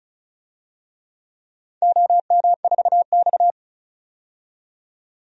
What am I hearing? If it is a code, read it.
Morse code, heard as a keyed tone, about 36 wpm, 700 Hz. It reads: OM4X
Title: DXCC-100 Call Signs - 4 Characters - 35wpm